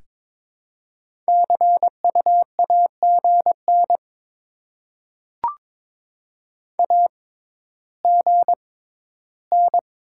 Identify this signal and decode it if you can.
Morse code, heard as a keyed tone, about 22 wpm, spaced as for 13 wpm, 700 Hz. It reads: CU AGN AGN
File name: Top 20 QSO Elements - 6x Character Spacing - 22wpm